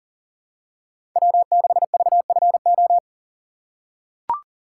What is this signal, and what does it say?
Morse code, heard as a keyed tone, about 40 wpm, 700 Hz. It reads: W6VFO
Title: USA Call Signs - 5 Characters - 2x Character Spacing - 40wpm